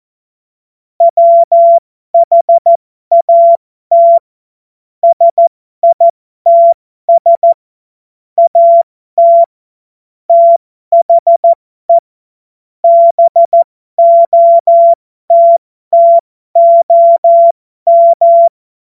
Morse code, heard as a keyed tone, about 14 wpm, spaced as for 12 wpm, 680 Hz. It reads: WHAT SITS AT THE BOTTOM
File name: Morsekurs / Witze